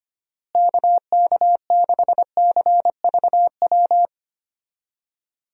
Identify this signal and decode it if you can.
Morse code, heard as a keyed tone, about 25 wpm, 700 Hz. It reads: KK6CVW
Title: USA Call Signs - 6 Characters - Repeated 3x - 25wpm